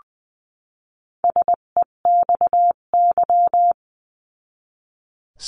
Morse code, heard as a keyed tone, about 19 wpm, 700 Hz. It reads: SEXY
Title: 258 Lesson 42c - Words 16-20wpm